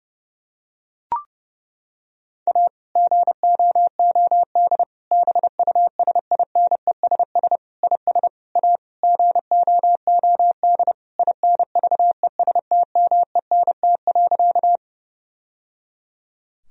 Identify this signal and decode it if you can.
Morse code, heard as a keyed tone, about 32 wpm, 700 Hz. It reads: A GOOD BUSINESS IS A GOOD INVESTMENT.
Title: Sentences from Top 1000 Words - Review Part 4 - 30wpm